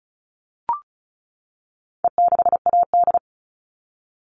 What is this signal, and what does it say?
Morse code, heard as a keyed tone, about 34 wpm, 700 Hz. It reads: E6AD
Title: All Cut Numbers - 4-Digit 35wpm